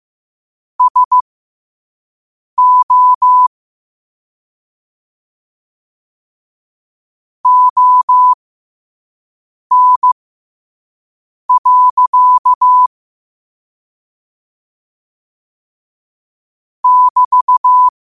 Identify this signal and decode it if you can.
Morse code, heard as a keyed tone, about 15 wpm, spaced as for 5 wpm, 1000 Hz. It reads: SO ON. <BT>
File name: qotdfiles / 5wpm